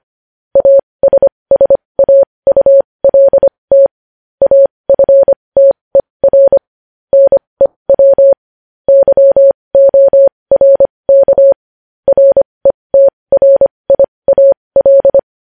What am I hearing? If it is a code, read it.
Morse code, heard as a keyed tone, about 25 wpm, spaced as for 20 wpm, 550 Hz. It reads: ASSAULT AFTER NEW YORK RETRIAL
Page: News Headlines in Morse Code at 20 WPM